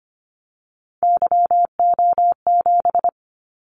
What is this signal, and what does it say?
Morse code, heard as a keyed tone, about 25 wpm, 700 Hz. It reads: YO7
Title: Callsign Prefixes for Top 25 DXCC - 25wpm